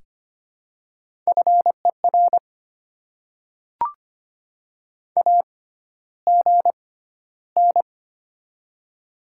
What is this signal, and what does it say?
Morse code, heard as a keyed tone, about 25 wpm, spaced as for 13 wpm, 700 Hz. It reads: FER AGN
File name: Top 20 QSO Elements - 6x Character Spacing - 25wpm